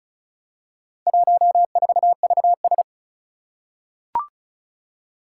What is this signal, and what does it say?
Morse code, heard as a keyed tone, about 36 wpm, 700 Hz. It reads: 14VS
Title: Four Challenging Letter-Number Combinations 35wpm